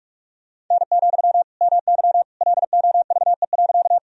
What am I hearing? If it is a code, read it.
Morse code, heard as a keyed tone, about 45 wpm, 700 Hz. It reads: N, MY LOVE.
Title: Sentences from Top 400 Words and Abbreviations 45wpm